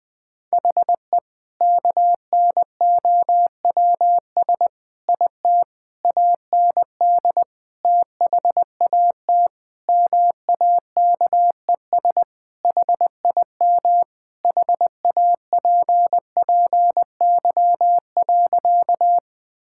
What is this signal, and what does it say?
Morse code, heard as a keyed tone, about 20 wpm, 700 Hz. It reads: HE KNOWS IT AND THAT MAKES HIM HAPPY.
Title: Sentences with 2-Word Phrases - 20wpm